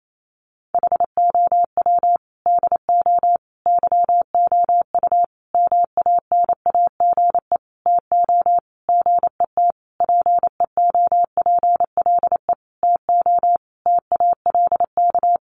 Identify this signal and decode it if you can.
Morse code, heard as a keyed tone, about 28 wpm, 700 Hz. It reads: HOW DO YOU MANAGE TO GET PEOPLE TO TALK
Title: Sentences from Top 1400 Words 28wpm